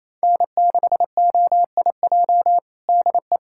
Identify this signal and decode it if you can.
Morse code, heard as a keyed tone, about 28 wpm, 700 Hz. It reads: N6OIJ DE
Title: USA Callsign DE Callsign - 5 Characters - 28wpm